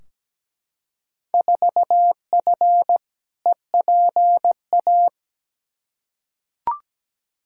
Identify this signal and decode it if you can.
Morse code, heard as a keyed tone, about 17 wpm, 700 Hz. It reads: 4F EPA
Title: Field Day Exchanges Encore - 17wpm